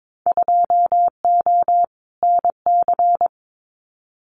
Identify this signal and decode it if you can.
Morse code, heard as a keyed tone, about 22 wpm, 700 Hz. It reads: 2O NC
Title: Winter Field Day Exchanges 22wpm